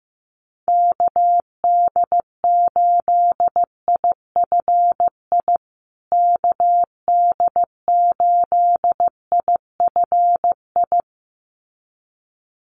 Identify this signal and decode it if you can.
Morse code, heard as a keyed tone, about 15 wpm, 700 Hz. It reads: KD8IFI KD8IFI
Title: USA Call Signs Repeated 2x 15wpm